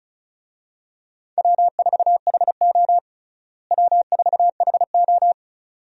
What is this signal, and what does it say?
Morse code, heard as a keyed tone, about 36 wpm, 700 Hz. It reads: W4HO W4HO
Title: USA Call Signs - 4 Characters - Repeated 2x - 35wpm